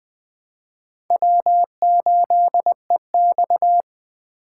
Morse code, heard as a keyed tone, about 20 wpm, 700 Hz. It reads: W8EX
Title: USA Call Signs - 4 Characters - 2x Character Spacing - 20wpm